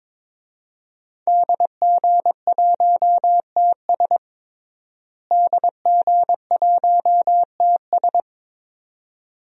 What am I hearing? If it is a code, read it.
Morse code, heard as a keyed tone, about 22 wpm, 700 Hz. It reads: DG1TS DG1TS
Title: DXCC-100 Call Signs - 5 Characters - Repeated 2x - 22wpm